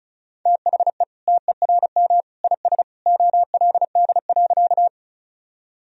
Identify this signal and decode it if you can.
Morse code, heard as a keyed tone, about 34 wpm, 700 Hz. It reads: THE TERM IS OLD.
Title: Sentences from Top 1000 Words - Review Part 4 - 35wpm